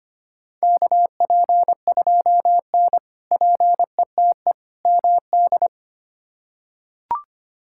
Morse code, heard as a keyed tone, about 25 wpm, 700 Hz. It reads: KP2N PETE MD
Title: K1USN SST QSO Practice - 25wpm